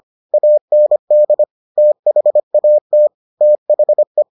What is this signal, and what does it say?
Morse code, heard as a keyed tone, about 25 wpm, 600 Hz. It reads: AND THAT THE